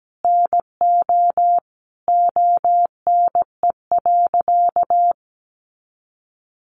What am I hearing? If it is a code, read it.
Morse code, heard as a keyed tone, about 17 wpm, 700 Hz. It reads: NO ONE.
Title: Sentences from Top 1100 Words 17wpm